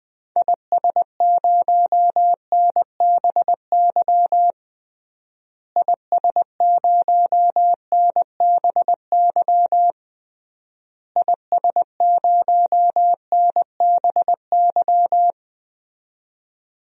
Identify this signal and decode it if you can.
Morse code, heard as a keyed tone, about 20 wpm, 700 Hz. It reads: IS0NBY IS0NBY IS0NBY
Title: DXCC-100 Call Signs - 6 Characters - Repeated 3x - 20wpm